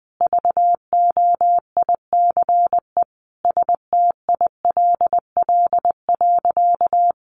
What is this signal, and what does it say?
Morse code, heard as a keyed tone, about 20 wpm, 700 Hz. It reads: VOICE STILL.
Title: Sentences from Top 500 Words 20wpm